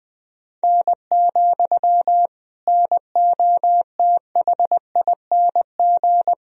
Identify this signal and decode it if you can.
Morse code, heard as a keyed tone, about 21 wpm, 700 Hz. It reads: N, NOTHING
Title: Sentences from Top 500 Words and Abbreviations 20wpm